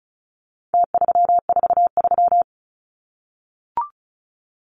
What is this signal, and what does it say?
Morse code, heard as a keyed tone, about 34 wpm, 700 Hz. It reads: T343
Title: Numbers - Contest Style - 4-Digit 35wpm